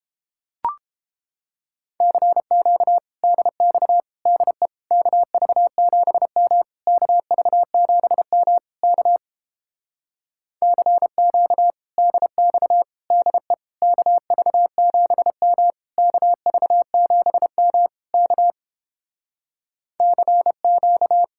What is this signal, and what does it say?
Morse code, heard as a keyed tone, about 29 wpm, 700 Hz. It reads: CQ DX DE KV7M KV7M K CQ DX DE KV7M KV7M K CQ